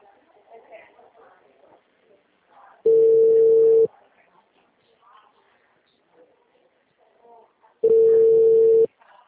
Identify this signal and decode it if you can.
Ringback tone (China, 450 Hz), with speech.